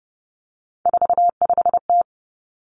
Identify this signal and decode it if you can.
Morse code, heard as a keyed tone, about 30 wpm, 700 Hz. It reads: 45T
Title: Three Challenging Letter-Number Combinations 30wpm